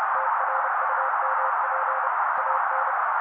Morse code, heard as a keyed tone, about 30 wpm, 530 Hz. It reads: TRUMP AN
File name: daily / 29wpm noise